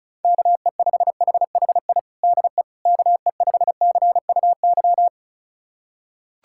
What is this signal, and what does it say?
Morse code, heard as a keyed tone, about 36 wpm, 700 Hz. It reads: KE5HHI DE KE5CUY
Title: USA Callsign DE Callsign - 6 Characters - 35wpm